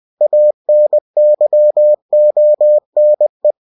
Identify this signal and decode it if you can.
Morse code, heard as a keyed tone, about 20 wpm, 600 Hz. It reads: ANYONE
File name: cw-books / winnie-the-pooh / winnie-20wpm-page-0120